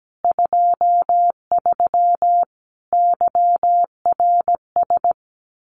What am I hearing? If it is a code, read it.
Morse code, heard as a keyed tone, about 17 wpm, 700 Hz. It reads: 23 YRS
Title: Common QSO Phrases Encore 17wpm